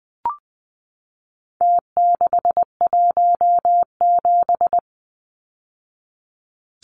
Morse code, heard as a keyed tone, about 20 wpm, 700 Hz. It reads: T617
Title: Numbers - Contest Style - 4-Digit 20wpm